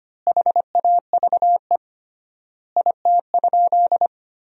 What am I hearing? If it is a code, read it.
Morse code, heard as a keyed tone, about 25 wpm, 700 Hz. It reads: HAVE IT?
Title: 3x Word-Spacing - Sentences from Top 100 Words - 25wpm